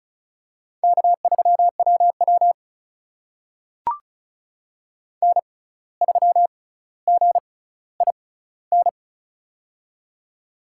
Morse code, heard as a keyed tone, about 36 wpm, spaced as for 17 wpm, 700 Hz. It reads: K3WW N3GIN
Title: USA Call Signs - 6x Character Spacing - 35wpm